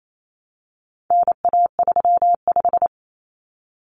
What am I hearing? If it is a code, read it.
Morse code, heard as a keyed tone, about 28 wpm, 700 Hz. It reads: NA35
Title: Common Cut Numbers - 4-Digit 28wpm